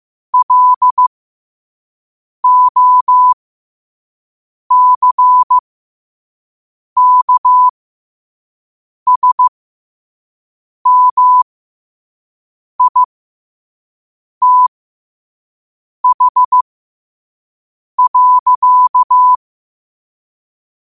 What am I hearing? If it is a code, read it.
Morse code, heard as a keyed tone, about 15 wpm, spaced as for 5 wpm, 1000 Hz. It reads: LOCKSMITH.
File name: 5wpm